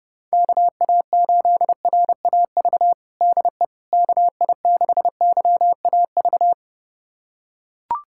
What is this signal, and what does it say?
Morse code, heard as a keyed tone, about 30 wpm, 700 Hz. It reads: KA8RAV DE KI6YAV
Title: USA Callsign DE Callsign - 6 Characters - 30wpm